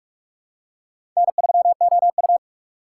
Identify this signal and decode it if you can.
Morse code, heard as a keyed tone, about 46 wpm, 700 Hz. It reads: N3OU
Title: USA Call Signs - 4 Characters - 4x Character Spacing - 45wpm